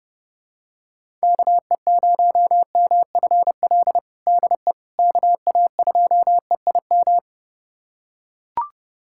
Morse code, heard as a keyed tone, about 30 wpm, 700 Hz. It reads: KE0MFL DE KA2EIM
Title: USA Callsign DE Callsign - 6 Characters - 30wpm